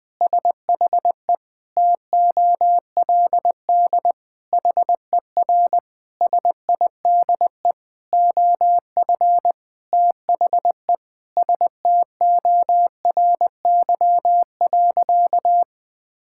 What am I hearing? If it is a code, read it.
Morse code, heard as a keyed tone, about 20 wpm, 700 Hz. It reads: SHE TOLD HER SIDE OF THE STORY.
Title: Sentences from Top 1000 Words - Review Part 6 - 20wpm